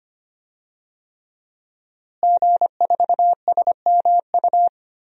Morse code, heard as a keyed tone, about 25 wpm, 700 Hz. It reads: G4SMU
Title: DXCC-100 Call Signs - 5 Characters - Repeated 3x - 25wpm